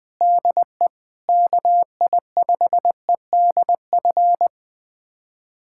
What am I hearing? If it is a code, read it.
Morse code, heard as a keyed tone, about 20 wpm, 700 Hz. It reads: DE KI5EDF
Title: USA Callsign DE Callsign 20wpm